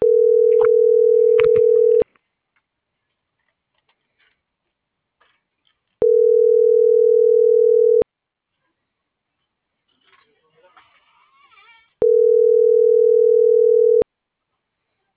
Ringback tone (North America), 440 + 480 Hz.